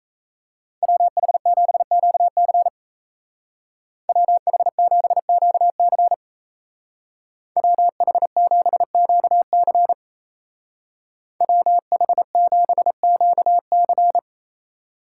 Morse code, heard as a keyed tone, about 37 wpm, 700 Hz. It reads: WH7QC WH7QC WH7QC WH7QC